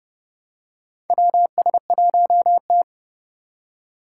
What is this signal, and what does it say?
Morse code, heard as a keyed tone, about 31 wpm, 700 Hz. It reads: WS1T
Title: USA Call Signs - 4 Characters - 3x Character Spacing - 30wpm